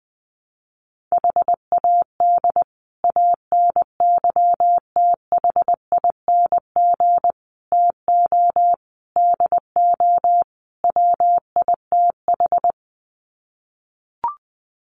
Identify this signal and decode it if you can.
Morse code, heard as a keyed tone, about 20 wpm, 700 Hz. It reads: HAD ANYTHING TO DO WITH
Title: Sets of 5 Words Encore 20wpm